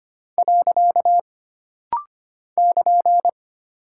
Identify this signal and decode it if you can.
Morse code, heard as a keyed tone, about 25 wpm, 700 Hz. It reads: . <KN>
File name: Single Punctuation-Prosign - Rapid-Fire - 25wpm